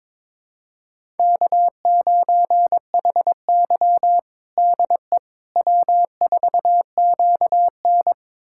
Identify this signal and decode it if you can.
Morse code, heard as a keyed tone, about 22 wpm, 700 Hz. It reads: K9HY DE W4QN
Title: USA Callsign DE Callsign - 4 Characters - 22wpm